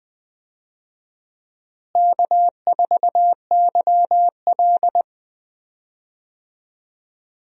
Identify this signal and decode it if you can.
Morse code, heard as a keyed tone, about 20 wpm, 700 Hz. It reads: K4YL
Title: USA Call Signs - 4 Characters - Repeated 2x - 20wpm